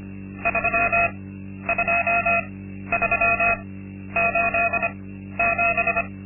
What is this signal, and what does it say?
Morse code, heard as a keyed tone, about 25 wpm, spaced as for 12 wpm, 680 Hz. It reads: 32387
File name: Linksmakalnis-2